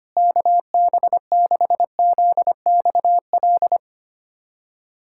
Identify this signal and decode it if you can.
Morse code, heard as a keyed tone, about 25 wpm, 700 Hz. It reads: KB6ZXL
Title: USA Callsign DE Callsign - 6 Characters - 25wpm